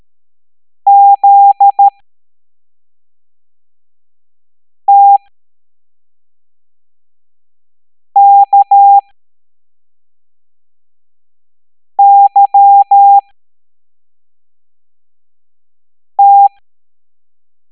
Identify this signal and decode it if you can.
Morse code, heard as a keyed tone, about 13 wpm, spaced as for 3 wpm, 800 Hz. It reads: ZTKYT